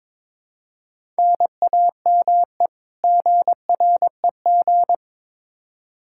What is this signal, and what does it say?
Morse code, heard as a keyed tone, about 22 wpm, 700 Hz. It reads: NAME GREG
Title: Practice QSO - Exchange 1 - Beginning - 22wpm